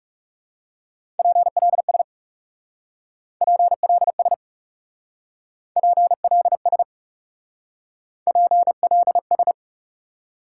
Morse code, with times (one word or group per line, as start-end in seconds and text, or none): 1.2-2.0 PLS
3.4-4.3 PLS
5.8-6.8 PLS
8.3-9.5 PLS